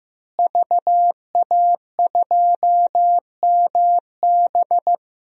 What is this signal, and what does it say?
Morse code, heard as a keyed tone, about 15 wpm, 700 Hz. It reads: VA2MB
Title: DXCC-100 Call Signs - 5 Characters - Repeated 3x - 15wpm